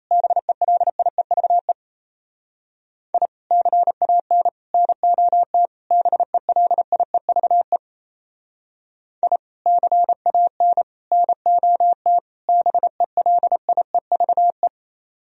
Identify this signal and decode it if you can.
Morse code, heard as a keyed tone, about 36 wpm, 700 Hz. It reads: BELIEVE I CAN NOT BELIEVE I CAN NOT BELIEVE